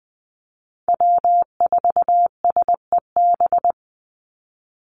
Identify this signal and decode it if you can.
Morse code, heard as a keyed tone, about 20 wpm, 700 Hz. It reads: W4SEB
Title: USA Call Signs 20wpm